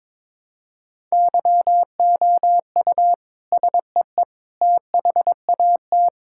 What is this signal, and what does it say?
Morse code, heard as a keyed tone, about 22 wpm, 700 Hz. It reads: YOU SEE THAT
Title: Sentences from Top 1000 Words - Review Part 6 - 22wpm